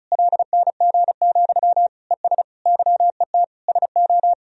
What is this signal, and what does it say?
Morse code, heard as a keyed tone, about 36 wpm, 700 Hz. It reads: LNG, ES YET SO
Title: Sentences from Top 500 Words and Abbreviations 35wpm